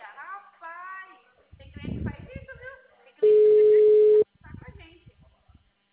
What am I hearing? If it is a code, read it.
Ringback tone (Europe, 425 Hz), with speech.
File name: out- -2001- - - .492